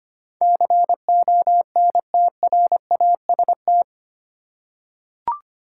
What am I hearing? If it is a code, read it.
Morse code, heard as a keyed tone, about 25 wpm, 690 Hz. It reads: CONTRAST
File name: 8 Letter Words - First 3 Letters Given - 25wpm